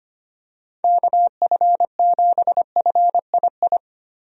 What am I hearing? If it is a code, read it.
Morse code, heard as a keyed tone, about 25 wpm, 700 Hz. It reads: KF7FII
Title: USA Call Signs - 6 Characters - Repeated 2x - 25wpm